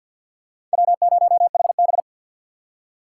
Morse code, heard as a keyed tone, about 51 wpm, 700 Hz. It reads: W0HB
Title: USA Call Signs - 3x Character Spacing - 50wpm